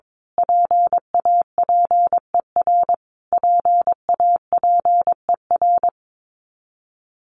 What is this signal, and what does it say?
Morse code, heard as a keyed tone, about 22 wpm, 700 Hz. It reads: PAPER PAPER
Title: Top 300 Words Repeated 22wpm